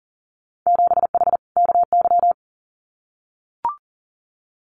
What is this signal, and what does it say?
Morse code, heard as a keyed tone, about 40 wpm, 700 Hz. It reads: 7H KY
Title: Winter Field Day Exchanges Encore 40wpm